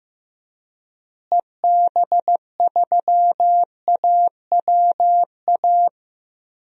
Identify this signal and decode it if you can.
Morse code, heard as a keyed tone, about 15 wpm, 700 Hz. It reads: EB3AWA
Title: DXCC-100 Call Signs - 6 Characters - Repeated 2x - 15wpm